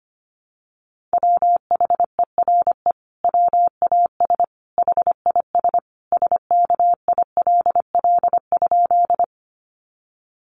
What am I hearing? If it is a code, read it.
Morse code, heard as a keyed tone, about 25 wpm, 700 Hz. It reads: WHERE WAS HIS SKILL?
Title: Sentences from Top 1200 Words 25wpm